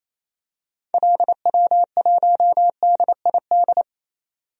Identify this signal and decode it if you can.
Morse code, heard as a keyed tone, about 28 wpm, 700 Hz. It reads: LW1DID
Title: DXCC-100 Call Signs - 6 Characters - 28wpm